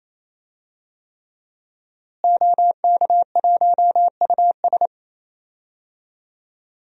Morse code, heard as a keyed tone, about 28 wpm, 700 Hz. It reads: OK1US
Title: DXCC-100 Call Signs - 5 Characters - Repeated 3x - 28wpm